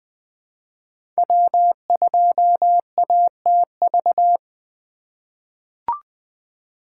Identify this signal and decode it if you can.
Morse code, heard as a keyed tone, about 20 wpm, 700 Hz. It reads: W2ATV
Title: USA Call Signs - 5 Characters - 4x Character Spacing - 20wpm